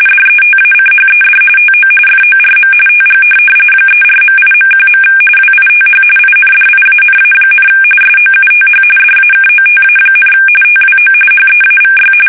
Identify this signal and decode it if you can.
RTTY, about 45.45 baud. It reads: DETACHED TO FORM TG 34.8.1. WITH INDIANA, SOUTH DAKOTA, TWO HEAVY
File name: fsk850 / USS-Massachusetts